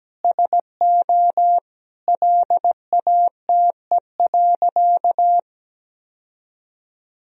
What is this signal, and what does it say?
Morse code, heard as a keyed tone, about 17 wpm, 700 Hz. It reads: SO LATE.
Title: Sentences from Top 1000 Words - Review Part 3 - 17wpm